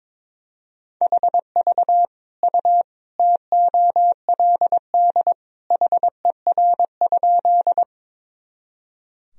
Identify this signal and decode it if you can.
Morse code, heard as a keyed tone, about 22 wpm, 700 Hz. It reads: HV U TOLD HER?
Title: Sentences from Top 400 Words and Abbreviations 22wpm